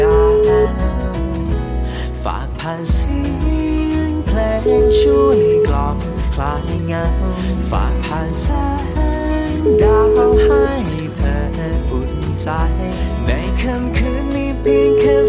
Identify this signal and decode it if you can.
Ringback tone (Europe, 425 Hz), with speech.